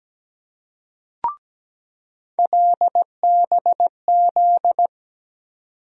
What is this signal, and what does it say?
Morse code, heard as a keyed tone, about 17 wpm, 700 Hz. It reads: LBZ
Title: USA Call Sign Postfixes - 3 Letters 17wpm